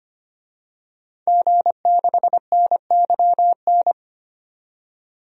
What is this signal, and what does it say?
Morse code, heard as a keyed tone, about 25 wpm, 700 Hz. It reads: G6NYN
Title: DXCC-100 Call Signs - 5 Characters - 25wpm